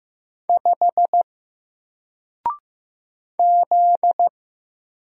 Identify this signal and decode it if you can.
Morse code, heard as a keyed tone, about 15 wpm, 700 Hz. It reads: 5 Z
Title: Single Letter-Number Alt 15wpm